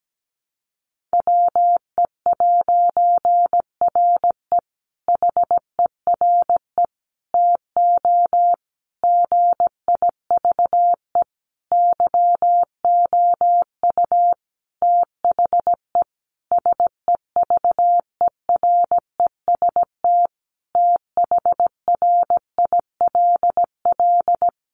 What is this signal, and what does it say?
Morse code, heard as a keyed tone, about 17 wpm, 700 Hz. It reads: WE'RE HERE TO GIVE YOU THE SEVEREST THRILL
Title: A Sound Of Thunder 17wpm